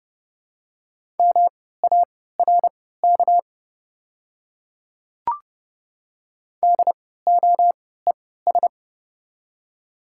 Morse code, heard as a keyed tone, about 30 wpm, spaced as for 17 wpm, 700 Hz. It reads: MARK DOES
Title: 3x Character Spacing - Top 300 Words - 30wpm